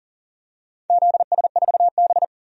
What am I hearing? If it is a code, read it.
Morse code, heard as a keyed tone, about 40 wpm, 700 Hz. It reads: ZS4B
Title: DXCC-100 Call Signs - 4 Characters - 40wpm